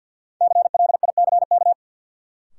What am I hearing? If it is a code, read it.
Morse code, heard as a keyed tone, about 50 wpm, 700 Hz. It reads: CLICK
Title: Sentences from Top 1100 Words - New Word Drill Alt - 50wpm